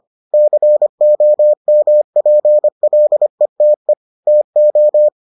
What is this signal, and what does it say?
Morse code, heard as a keyed tone, about 25 wpm, 600 Hz. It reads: COMPLETE TO